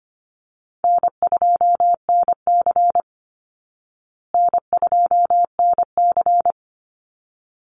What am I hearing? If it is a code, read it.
Morse code, heard as a keyed tone, about 25 wpm, 700 Hz. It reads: N2NC N2NC
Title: Easy Call Signs Repeated 2x - 25wpm